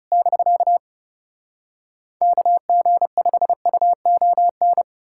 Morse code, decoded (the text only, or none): <BK> KG5UON